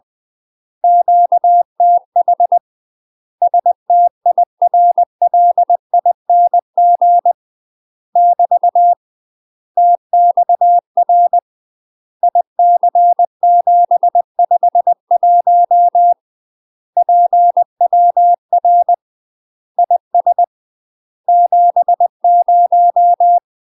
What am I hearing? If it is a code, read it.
Morse code, heard as a keyed tone, about 20 wpm, 700 Hz. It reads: QTH STIRLING <BT> TXR IC751 PWR IS 70
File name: UK MASTER QSOs TRAINING / 20WPM / UK MASTER QSOs 0042